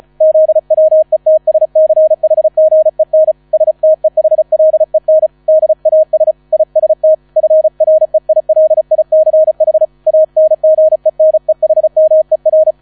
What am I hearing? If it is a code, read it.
Morse code, heard as a keyed tone, about 34 wpm, 630 Hz. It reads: ZWETSCHGEN STEHLEN DAS IST FREILICH ANGENEHMER